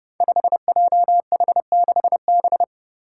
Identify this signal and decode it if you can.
Morse code, heard as a keyed tone, about 30 wpm, 700 Hz. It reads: 5JH6B
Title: Five Challenging Letter-Number Combinations 30wpm